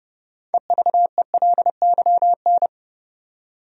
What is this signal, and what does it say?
Morse code, heard as a keyed tone, about 30 wpm, 700 Hz. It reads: EVELYN
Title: USA Common Names - 30wpm